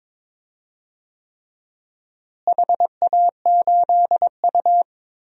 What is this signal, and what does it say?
Morse code, heard as a keyed tone, about 22 wpm, 700 Hz. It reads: HA8U
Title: DXCC-100 Call Signs - 4 Characters - Repeated 3x - 22wpm